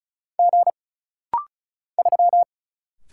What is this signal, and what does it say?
Morse code, heard as a keyed tone, about 34 wpm, 700 Hz. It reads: G 3
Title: Single Letter-Number Alt - Rapid-Fire 35wpm